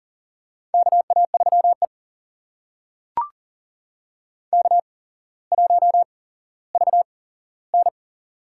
Morse code, heard as a keyed tone, about 40 wpm, spaced as for 17 wpm, 700 Hz. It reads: KA3E K1VN
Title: USA Call Signs - 4 Characters - 8x Character Spacing - 40wpm